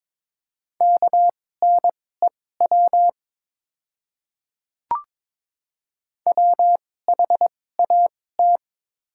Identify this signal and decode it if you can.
Morse code, heard as a keyed tone, about 22 wpm, spaced as for 16 wpm, 690 Hz. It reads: KNEW WHAT
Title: 2x Character Spacing - Top 500 Words - 22wpm